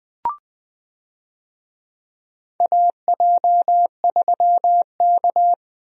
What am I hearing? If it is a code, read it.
Morse code, heard as a keyed tone, about 21 wpm, 700 Hz. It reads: AJ3K
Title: USA Call Signs - 4 Characters - 20wpm